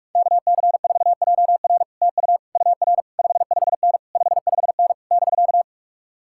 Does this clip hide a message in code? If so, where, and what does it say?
Yes: Morse code from 0.2 to 5.6 s, heard as a keyed tone, about 49 wpm, 700 Hz. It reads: KC4JR TU UR 55N 55N <BK>